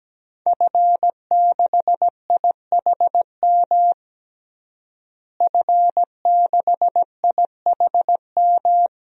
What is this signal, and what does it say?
Morse code, heard as a keyed tone, about 17 wpm, 700 Hz. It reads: F6IHM F6IHM
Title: DXCC-100 Call Signs - 5 Characters - Repeated 3x - 17wpm